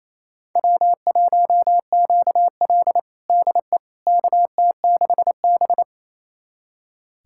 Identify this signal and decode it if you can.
Morse code, heard as a keyed tone, about 28 wpm, 700 Hz. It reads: W1QL DE KT6B
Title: USA Callsign DE Callsign - 4 Characters - 28wpm